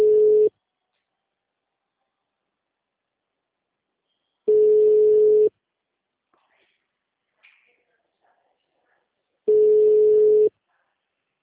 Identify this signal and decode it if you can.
Ringback tone (Europe, 425 Hz), with speech.